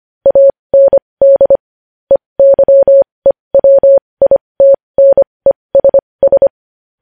Morse code, heard as a keyed tone, about 25 wpm, spaced as for 20 wpm, 550 Hz. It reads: AND EYEWITNESS
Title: News Headlines at 20